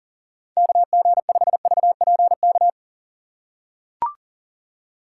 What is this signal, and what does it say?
Morse code, heard as a keyed tone, about 40 wpm, 700 Hz. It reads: KG5VPK
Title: USA Call Signs - 2x Character Spacing - 40wpm